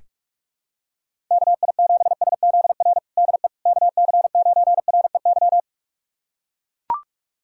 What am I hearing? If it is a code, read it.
Morse code, heard as a keyed tone, about 41 wpm, 700 Hz. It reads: KI7SZR DE KC9REY
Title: USA Callsign DE Callsign - 6 Characters - 45wpm